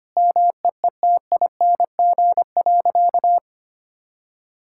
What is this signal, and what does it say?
Morse code, heard as a keyed tone, about 25 wpm, 700 Hz. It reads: MEETING.
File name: practice / 25wpm / Sentences with 3-Word Phrases - 25wpm